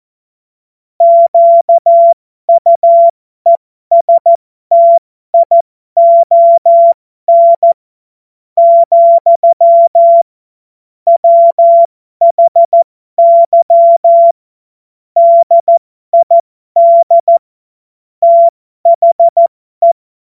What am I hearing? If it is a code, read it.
Morse code, heard as a keyed tone, about 14 wpm, spaced as for 12 wpm, 680 Hz. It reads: QUESTION, WHY DID THE
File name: Morsekurs / Witze